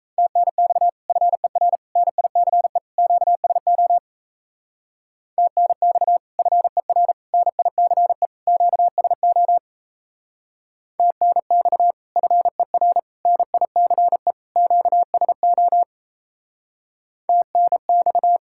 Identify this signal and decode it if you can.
Morse code, heard as a keyed tone, about 32 wpm, 700 Hz. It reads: TNX FER NICE QSO TNX FER NICE QSO TNX FER NICE QSO TNX
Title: Speed-Racing Common QSO Phrases Encore - 28wpm